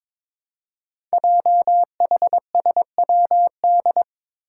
Morse code, heard as a keyed tone, about 22 wpm, 700 Hz. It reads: JHSWD
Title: Code Groups - 5 Letters - 22wpm